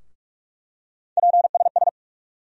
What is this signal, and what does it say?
Morse code, heard as a keyed tone, about 47 wpm, 700 Hz. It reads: PSS
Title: USA Call Sign Postfixes - 3 Letters 45wpm